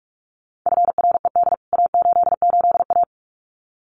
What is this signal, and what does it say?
Morse code, heard as a keyed tone, about 44 wpm, 700 Hz. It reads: FRED A88A
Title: CWT Exchange 45wpm